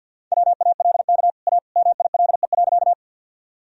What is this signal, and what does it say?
Morse code, heard as a keyed tone, about 52 wpm, 700 Hz. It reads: WALK A MILE.
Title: Sentences from Top 1000 Words - Review Part 1 - 50wpm